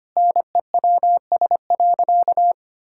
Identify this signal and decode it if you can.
Morse code, heard as a keyed tone, about 25 wpm, 700 Hz. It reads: NEWS.
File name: practice / 25wpm / Sentences with 2-Word Phrases - 25wpm